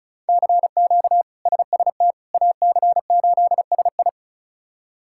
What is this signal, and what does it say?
Morse code, heard as a keyed tone, about 36 wpm, 700 Hz. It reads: CQ SST AC8SI
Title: K1USN SST QSO Practice - 35wpm